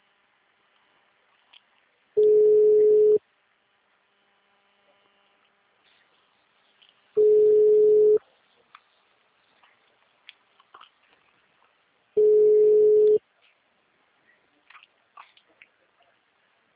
Ringback tone (Europe, 425 Hz).